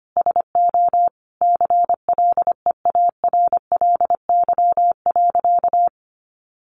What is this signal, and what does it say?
Morse code, heard as a keyed tone, about 25 wpm, 700 Hz. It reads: SO CLEARLY.
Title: Sentences from Top 1300 Words 25wpm